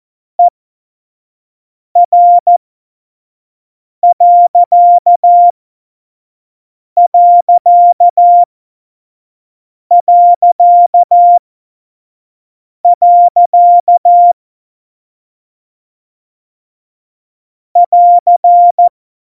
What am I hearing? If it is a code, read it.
Morse code, heard as a keyed tone, about 14 wpm, spaced as for 5 wpm, 700 Hz. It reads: ER.... <AR>
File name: Morsekurs / Witze / Witz 08. Januar 2019, 25 F70